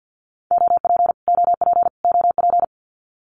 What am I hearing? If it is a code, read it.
Morse code, heard as a keyed tone, about 56 wpm, 700 Hz. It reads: OP OP OP
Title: Abbreviations Repeated 3x - 50wpm